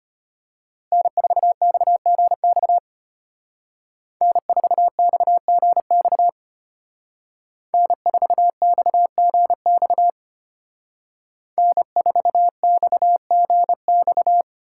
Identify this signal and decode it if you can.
Morse code, heard as a keyed tone, about 30 wpm, 700 Hz. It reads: N4XGX N4XGX N4XGX N4XGX
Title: Speed Racing - USA 1x3 Call Signs 25wpm